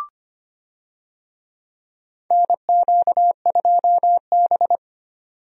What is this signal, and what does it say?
Morse code, heard as a keyed tone, about 25 wpm, 700 Hz. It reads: NQ2B
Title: USA Call Signs - 4 Characters - 25wpm